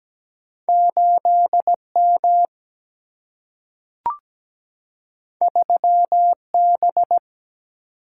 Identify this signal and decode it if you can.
Morse code, heard as a keyed tone, about 17 wpm, 700 Hz. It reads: 8M 3B
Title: Single-Letter Single-Number 17wpm